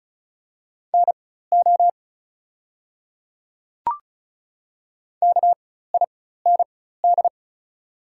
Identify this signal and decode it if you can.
Morse code, heard as a keyed tone, about 36 wpm, spaced as for 16 wpm, 700 Hz. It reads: NO KIND
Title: 4x Character Spacing - Top 200 Words - 35wpm